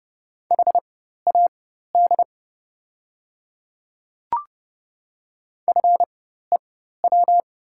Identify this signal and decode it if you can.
Morse code, heard as a keyed tone, about 30 wpm, spaced as for 14 wpm, 700 Hz. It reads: HAD FEW
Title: 4x Character Spacing - Top 300 Words - 30wpm